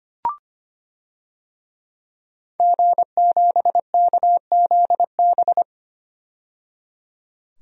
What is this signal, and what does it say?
Morse code, heard as a keyed tone, about 25 wpm, 700 Hz. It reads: G7KZB
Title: DXCC-100 Call Signs - 5 Characters - 25wpm